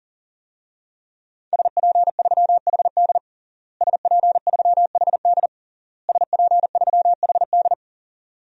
Morse code, heard as a keyed tone, about 37 wpm, 700 Hz. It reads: SP3HD SP3HD SP3HD
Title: DXCC-100 Call Signs - 5 Characters - Repeated 3x - 40wpm